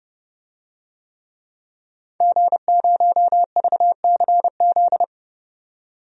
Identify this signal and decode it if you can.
Morse code, heard as a keyed tone, about 30 wpm, 700 Hz. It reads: G0VCZ
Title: DXCC-100 Call Signs - 5 Characters - 30wpm